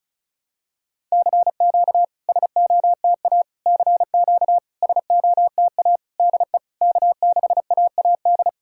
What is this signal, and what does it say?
Morse code, heard as a keyed tone, about 36 wpm, 700 Hz. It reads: CQ SOTA CQ SOTA DE K6AAD
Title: SOTA QSO Practice 35wpm